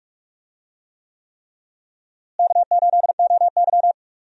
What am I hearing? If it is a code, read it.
Morse code, heard as a keyed tone, about 46 wpm, 700 Hz. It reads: K8OY
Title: USA Call Signs - 4 Characters - Repeated 3x - 45wpm